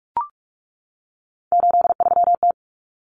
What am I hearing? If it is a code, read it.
Morse code, heard as a keyed tone, about 44 wpm, 700 Hz. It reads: 83T
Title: Common Cut Numbers - 3-Digit 45wpm